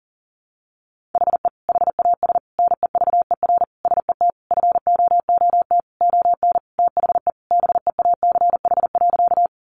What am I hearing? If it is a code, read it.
Morse code, heard as a keyed tone, about 40 wpm, 700 Hz. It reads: HE HAS NEVER SET FOOT ON THE BEACH.
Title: Sentences from Top 1500 Words 40wpm